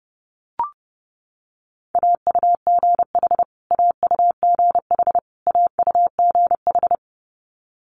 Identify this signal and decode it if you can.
Morse code, heard as a keyed tone, about 30 wpm, 700 Hz. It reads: AUGH AUGH AUGH
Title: Multi-letter Phonemes Repeated 3x 30wpm